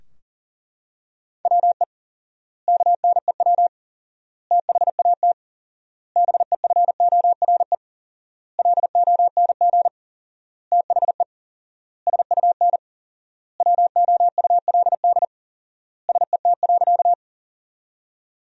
Morse code, heard as a keyed tone, about 40 wpm, 700 Hz. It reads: WE KNEW THAT BEFORE LONG THE SUN WOULD SET.
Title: 4x Word-Spacing - Sentences from Top 400 Words - 40wpm